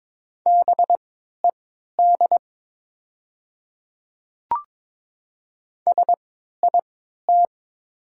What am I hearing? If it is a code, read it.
Morse code, heard as a keyed tone, about 22 wpm, spaced as for 12 wpm, 700 Hz. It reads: BED SIT
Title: 3x Character Spacing - Top 500 Words - 22wpm